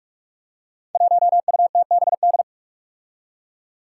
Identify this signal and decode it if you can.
Morse code, heard as a keyed tone, about 44 wpm, 700 Hz. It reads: 1UTBD
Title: Five Challenging Letter-Number Combinations 45wpm